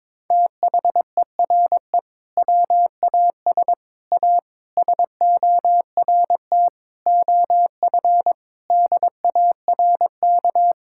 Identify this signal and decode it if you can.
Morse code, heard as a keyed tone, about 22 wpm, 700 Hz. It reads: THERE WAS A SORT OF DARK
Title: Sentences from Top 1600 Words 22wpm